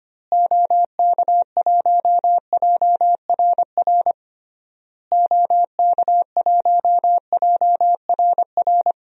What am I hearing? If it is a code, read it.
Morse code, heard as a keyed tone, about 25 wpm, 700 Hz. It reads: OK1JRR OK1JRR
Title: DXCC-100 Call Signs - 6 Characters - Repeated 3x - 25wpm